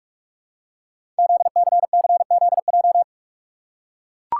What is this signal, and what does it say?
Morse code, heard as a keyed tone, about 44 wpm, 700 Hz. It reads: ZCCZJ
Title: Code Groups - 5 Letters - 45wpm